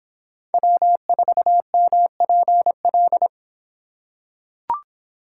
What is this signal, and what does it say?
Morse code, heard as a keyed tone, about 26 wpm, 700 Hz. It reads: W4MPL